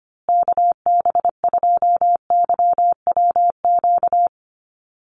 Morse code, heard as a keyed tone, about 25 wpm, 700 Hz. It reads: KB2YWQ
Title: USA Call Signs Repeated 2x 25wpm